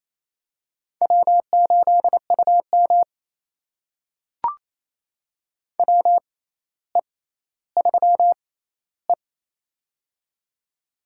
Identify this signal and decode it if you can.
Morse code, heard as a keyed tone, about 28 wpm, spaced as for 14 wpm, 700 Hz. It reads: W8UM WE3E
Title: USA Call Signs - 4 Characters - 6x Character Spacing - 28wpm